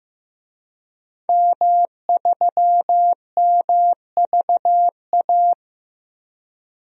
Morse code, heard as a keyed tone, about 15 wpm, 700 Hz. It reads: M3MVA
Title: DXCC-100 Call Signs - 5 Characters - Repeated 2x - 15wpm